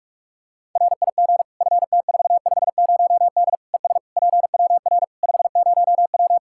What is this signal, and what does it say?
Morse code, heard as a keyed tone, about 41 wpm, 700 Hz. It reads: RIG FT450D ES PWR 50W
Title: K1USN SST QSO Practice 45wpm